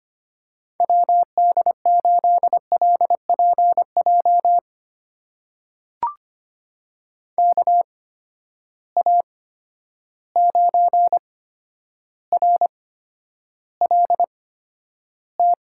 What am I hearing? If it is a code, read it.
Morse code, heard as a keyed tone, about 25 wpm, spaced as for 11 wpm, 700 Hz. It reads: WD8LPJ KA9RLT
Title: USA Call Signs - 6 Characters - 8x Character Spacing - 25wpm